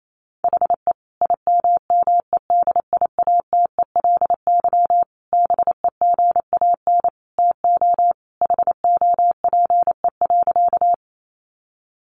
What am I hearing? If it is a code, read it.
Morse code, heard as a keyed tone, about 28 wpm, 700 Hz. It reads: HE IMMEDIATELY BEGAN TO HOPE.
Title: Sentences from Top 1100 Words 28wpm